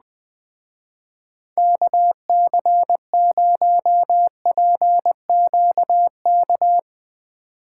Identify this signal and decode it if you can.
Morse code, heard as a keyed tone, about 20 wpm, 700 Hz. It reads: KC0PQK